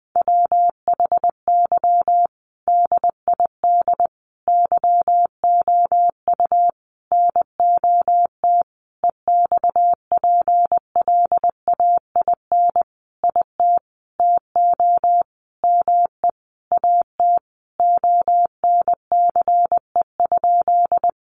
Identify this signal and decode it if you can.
Morse code, heard as a keyed tone, about 20 wpm, 700 Hz. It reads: WHY DID YOU NOT EXPLAIN IT TO ME AT ONCE?
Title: Sentences from Top 1100 Words 20wpm